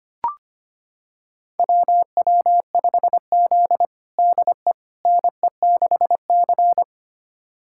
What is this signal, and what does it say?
Morse code, heard as a keyed tone, about 25 wpm, 700 Hz. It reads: WW5Z DE NE6C
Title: USA Callsign DE Callsign - 4 Characters - 25wpm